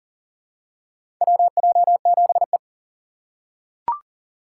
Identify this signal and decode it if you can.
Morse code, heard as a keyed tone, about 40 wpm, 700 Hz. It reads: WJ7E
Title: USA Call Signs - 4 Characters - 6x Character Spacing - 40wpm